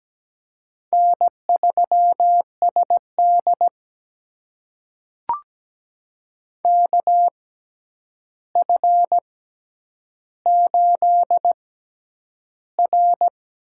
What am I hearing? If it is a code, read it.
Morse code, heard as a keyed tone, about 17 wpm, spaced as for 9 wpm, 700 Hz. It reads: N3SD KF8R
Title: USA Call Signs - 4 Characters - 6x Character Spacing - 17wpm